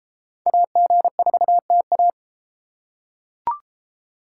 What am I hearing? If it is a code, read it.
Morse code, heard as a keyed tone, about 32 wpm, 700 Hz. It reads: AG4TA